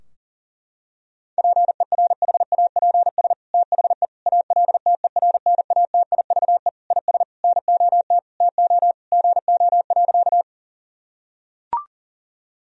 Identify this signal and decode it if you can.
Morse code, heard as a keyed tone, about 37 wpm, 700 Hz. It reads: PERHAPS THE ALTERNATIVE IS NOT TO GO.
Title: Sentences from Top 1000 Words - Review Part 4 - 40wpm